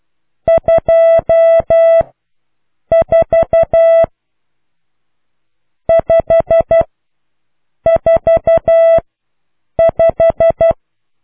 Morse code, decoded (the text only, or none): 2 4 5 4 5